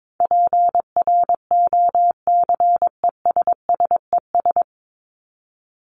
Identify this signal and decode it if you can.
Morse code, heard as a keyed tone, about 22 wpm, 700 Hz. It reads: PROCESSES
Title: ES:S Suffix - Top 100 Words with Base Words 22wpm